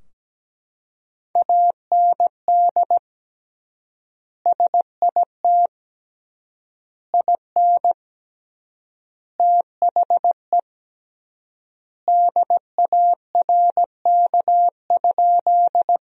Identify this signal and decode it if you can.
Morse code, heard as a keyed tone, about 17 wpm, 700 Hz. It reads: AND SIT IN THE DARK?
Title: 3x Word-Spacing - Sentences from Top 500 Words - 17wpm